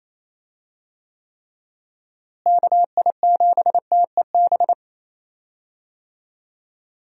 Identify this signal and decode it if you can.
Morse code, heard as a keyed tone, about 28 wpm, 700 Hz. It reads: KI7TEB
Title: USA Call Signs - 6 Characters - Repeated 3x - 28wpm